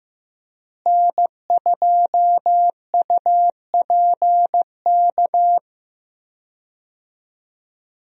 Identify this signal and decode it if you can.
Morse code, heard as a keyed tone, about 15 wpm, 700 Hz. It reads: N2UPK
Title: USA Call Signs - 5 Characters - Repeated 3x - 15wpm